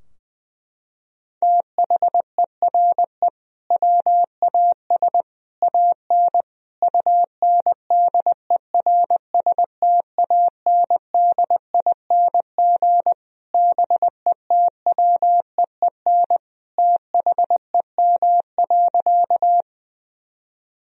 Morse code, heard as a keyed tone, about 20 wpm, 700 Hz. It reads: THERE WAS AN UNDERSTANDING BETWEEN THEM.
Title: Sentences from Top 1000 Words - Review Part 3 - 20wpm